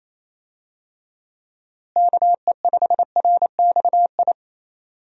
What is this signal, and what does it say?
Morse code, heard as a keyed tone, about 28 wpm, 700 Hz. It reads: KE5RXI
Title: USA Call Signs - 6 Characters - Repeated 3x - 28wpm